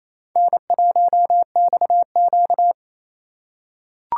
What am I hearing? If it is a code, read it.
Morse code, heard as a keyed tone, about 28 wpm, 700 Hz. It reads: N1XQ
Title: USA Call Signs - 2x Character Spacing - 28wpm